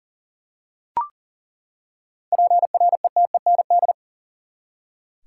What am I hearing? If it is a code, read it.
Morse code, heard as a keyed tone, about 40 wpm, 700 Hz. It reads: PRETEND
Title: Seven Letter Words - 40wpm